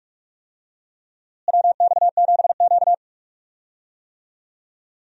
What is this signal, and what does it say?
Morse code, heard as a keyed tone, about 46 wpm, 700 Hz. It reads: WX7Q
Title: USA Call Signs - 4 Characters - Repeated 3x - 45wpm